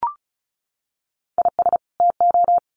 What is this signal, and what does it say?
Morse code, heard as a keyed tone, about 34 wpm, 700 Hz. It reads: IS TO
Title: Sets of 2 Words 35wpm